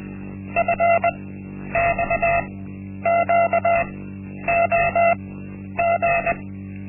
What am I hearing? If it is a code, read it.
Morse code, heard as a keyed tone, about 20 wpm, spaced as for 10 wpm, 680 Hz. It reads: FXQOG
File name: Linksmakalnis-1